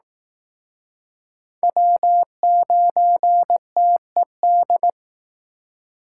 Morse code, heard as a keyed tone, about 19 wpm, 700 Hz. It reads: W9TED